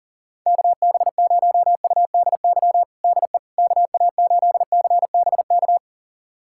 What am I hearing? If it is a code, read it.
Morse code, heard as a keyed tone, about 40 wpm, 700 Hz. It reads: KB0UDY DE KA8CBK
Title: USA Callsign DE Callsign - 6 Characters - 40wpm